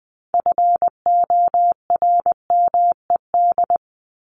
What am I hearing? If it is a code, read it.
Morse code, heard as a keyed tone, about 20 wpm, 700 Hz. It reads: FORMED
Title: ED Suffix - Top 100 Words with Base Words 20wpm